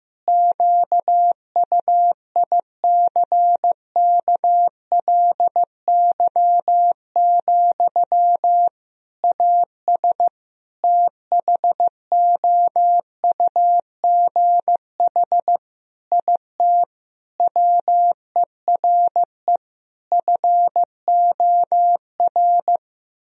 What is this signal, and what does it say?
Morse code, heard as a keyed tone, about 16 wpm, 700 Hz. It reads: QUICKLY, AS THOUGH IT WERE FOR
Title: Sentences from Top 1000 Words - Review Part 2 - 15wpm